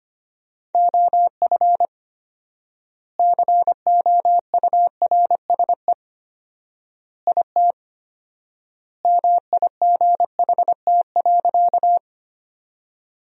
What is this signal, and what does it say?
Morse code, heard as a keyed tone, about 25 wpm, 700 Hz. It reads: OF COURSE IT MIGHT.
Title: 4x Word-Spacing - Sentences from Top 500 Words - 25wpm